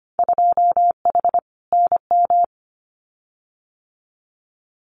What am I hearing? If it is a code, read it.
Morse code, heard as a keyed tone, about 25 wpm, 700 Hz. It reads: 2H NM
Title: Winter Field Day Exchanges Encore 25wpm